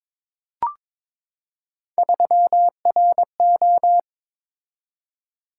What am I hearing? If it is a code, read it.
Morse code, heard as a keyed tone, about 22 wpm, 700 Hz. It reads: 3RO
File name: practice / 22wpm / Three Letter-Number Combinations - 22wpm